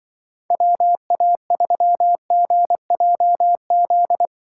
Morse code, heard as a keyed tone, about 24 wpm, 700 Hz. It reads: WA3GJZ
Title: Speed Racing - USA Call Signs 20wpm